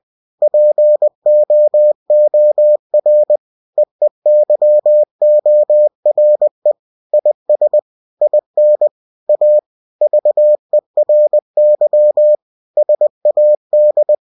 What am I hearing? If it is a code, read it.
Morse code, heard as a keyed tone, about 20 wpm, 600 Hz. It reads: POOR EEYORE IS IN A VERY SAD
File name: cw-books / winnie-the-pooh / winnie-20wpm-page-0048